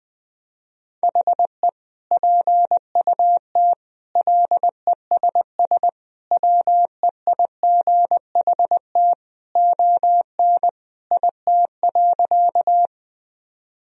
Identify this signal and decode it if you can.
Morse code, heard as a keyed tone, about 21 wpm, 700 Hz. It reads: HE PUT LESS WEIGHT ON IT.
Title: Sentences from Top 1000 Words - Review Part 3 - 20wpm